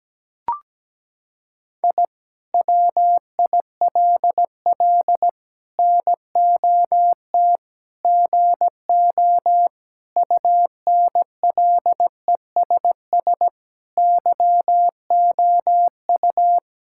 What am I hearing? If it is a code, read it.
Morse code, heard as a keyed tone, about 17 wpm, 700 Hz. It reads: I WILL NOT GO UNLESS YOU
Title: Sentences from Top 1000 Words - Review Part 6 - 17wpm